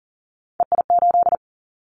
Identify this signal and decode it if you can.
Morse code, heard as a keyed tone, about 42 wpm, 700 Hz. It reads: EI8
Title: Callsign Prefixes for Top 50 DXCC - 40wpm